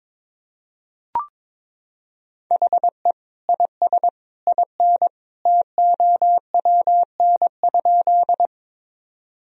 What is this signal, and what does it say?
Morse code, heard as a keyed tone, about 21 wpm, 700 Hz. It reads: HE IS IN TOWN?
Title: Sentences from Top 1000 Words - Review Part 2 - 22wpm